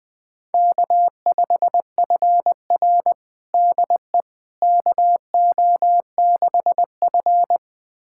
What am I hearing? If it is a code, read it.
Morse code, heard as a keyed tone, about 20 wpm, 700 Hz. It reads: K5FR DE KO6F
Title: USA Callsign DE Callsign - 4 Characters - 20wpm